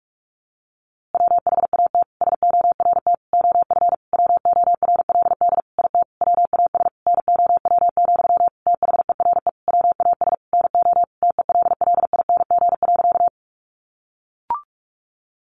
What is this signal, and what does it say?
Morse code, heard as a keyed tone, about 44 wpm, 700 Hz. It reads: WHAT SORT OF WORLD IT WAS NOW, THERE WAS NO TELLING.
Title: A Sound Of Thunder 45wpm